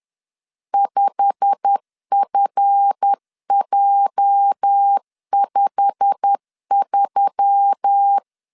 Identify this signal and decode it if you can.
Morse code, heard as a keyed tone, about 10 wpm, 800 Hz. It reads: 5FJ53
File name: LEZIONI / 8 YJ45 / 8 YJ45 10wpm 2549